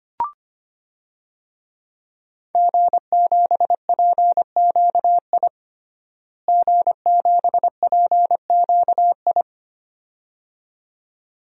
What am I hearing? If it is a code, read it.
Morse code, heard as a keyed tone, about 25 wpm, 700 Hz. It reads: G7PQI G7PQI
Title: DXCC-100 Call Signs - 5 Characters - Repeated 2x - 25wpm